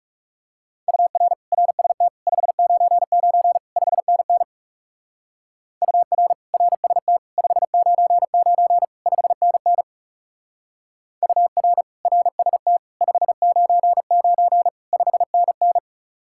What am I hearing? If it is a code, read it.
Morse code, heard as a keyed tone, about 36 wpm, 700 Hz. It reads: UR RST 599 5NN UR RST 599 5NN UR RST 599 5NN
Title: Speed-Racing Common QSO Phrases - 30wpm